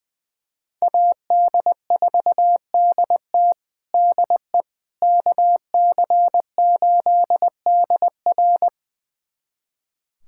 Morse code, heard as a keyed tone, about 20 wpm, 700 Hz. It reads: AD4DT DE KC8DR
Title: USA Callsign DE Callsign - 5 Characters - 20wpm